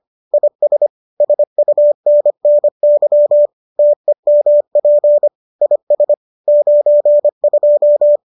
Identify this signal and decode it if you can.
Morse code, heard as a keyed tone, about 25 wpm, 590 Hz. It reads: IS SUNNY TEMP IS 92